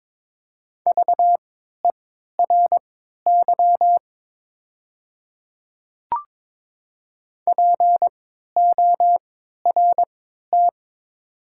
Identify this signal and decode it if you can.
Morse code, heard as a keyed tone, about 22 wpm, spaced as for 13 wpm, 700 Hz. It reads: VERY PORT
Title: 3x Character Spacing - Top 300 Words - 22wpm